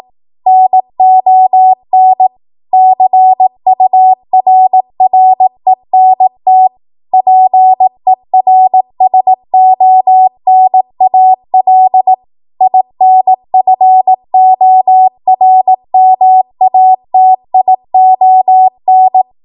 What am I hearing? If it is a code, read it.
Morse code, heard as a keyed tone, about 18 wpm, 750 Hz. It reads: NON CURRENT PERSONAL INFORMATION
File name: Morse / 18WPM